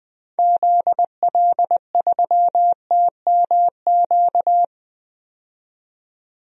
Morse code, heard as a keyed tone, about 20 wpm, 700 Hz. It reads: ZL3TMQ
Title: DXCC-100 Call Signs - 6 Characters - 20wpm